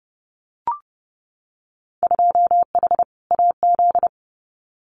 Morse code, heard as a keyed tone, about 30 wpm, 700 Hz. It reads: 2H AZ
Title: Winter Field Day Exchanges 30wpm